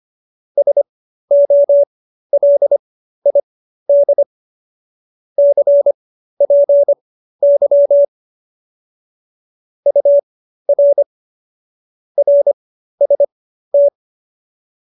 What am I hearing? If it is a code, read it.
Morse code, heard as a keyed tone, about 25 wpm, spaced as for 13 wpm, 570 Hz. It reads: SOLID CPY UR RST